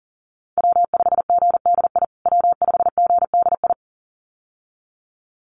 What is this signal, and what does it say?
Morse code, heard as a keyed tone, about 40 wpm, 700 Hz. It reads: W5GDI W5GDI
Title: USA Call Signs Repeated 2x 40wpm